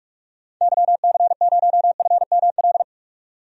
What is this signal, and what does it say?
Morse code, heard as a keyed tone, about 42 wpm, 700 Hz. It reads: YC0FML
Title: DXCC-100 Call Signs - 6 Characters - 45wpm